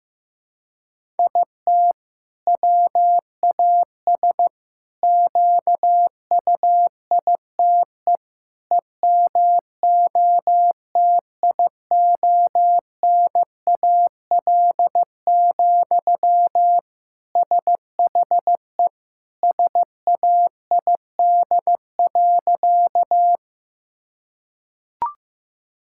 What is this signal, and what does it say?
Morse code, heard as a keyed tone, about 15 wpm, 700 Hz. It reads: IT WAS QUITE EMOTIONAL, SHE SAID.
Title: Sentences from Top 1000 Words - Review Part 5 - 15wpm